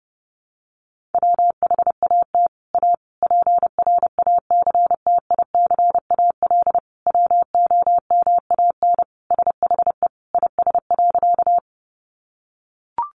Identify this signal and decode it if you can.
Morse code, heard as a keyed tone, about 30 wpm, 700 Hz. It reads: WHAT A PRACTICAL WOMAN SHE IS.
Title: Sentences from Top 1400 Words 30wpm